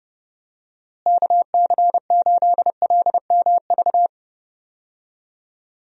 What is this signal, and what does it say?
Morse code, heard as a keyed tone, about 30 wpm, 700 Hz. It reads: KC8LMV
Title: USA Call Signs - 6 Characters - Repeated 2x - 30wpm